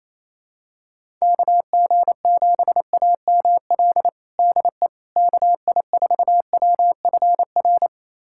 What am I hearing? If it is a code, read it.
Morse code, heard as a keyed tone, about 28 wpm, 700 Hz. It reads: KG7AML DE KI4WFR
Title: USA Callsign DE Callsign 28wpm